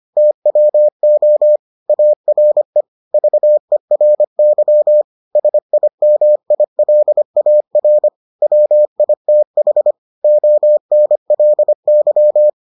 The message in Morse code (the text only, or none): TWO ARE VERY SIMILAR WITH ONLY